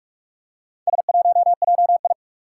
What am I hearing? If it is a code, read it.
Morse code, heard as a keyed tone, about 44 wpm, 700 Hz. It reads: S1JI
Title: Four Challenging Letter-Number Combinations 45wpm